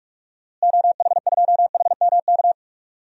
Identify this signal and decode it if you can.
Morse code, heard as a keyed tone, about 41 wpm, 700 Hz. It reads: OH2HMK
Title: DXCC-100 Call Signs - 6 Characters - Repeated 3x - 45wpm